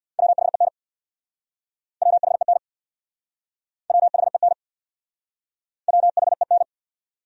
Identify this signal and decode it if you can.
Morse code, heard as a keyed tone, about 66 wpm, 700 Hz. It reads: WHEN WHEN WHEN WHEN